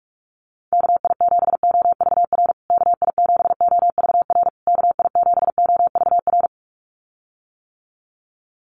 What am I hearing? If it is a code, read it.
Morse code, heard as a keyed tone, about 44 wpm, 700 Hz. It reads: KI7OVR KI7OVR KI7OVR
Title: USA Call Signs Repeated 3x 45wpm